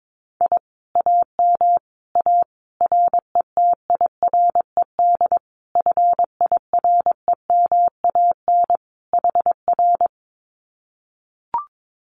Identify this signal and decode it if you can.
Morse code, heard as a keyed tone, about 22 wpm, 700 Hz. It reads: I AM A RETIRED FIREMAN HR
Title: Common QSO Phrases Encore 22wpm